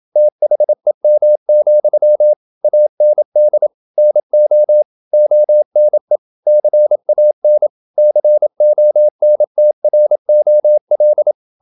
Morse code, heard as a keyed tone, about 27 wpm, 600 Hz. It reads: THEM, AND NO ONE CAN CONTROL
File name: Never 0027wpm